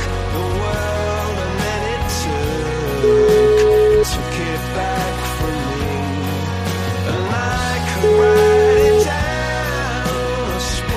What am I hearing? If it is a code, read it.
Ringback tone (Europe, 425 Hz), with noise.